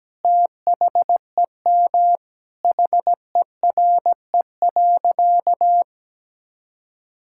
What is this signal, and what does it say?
Morse code, heard as a keyed tone, about 17 wpm, 700 Hz. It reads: THEM HERE.
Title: Sentences from Top 1000 Words - Review Part 5 - 17wpm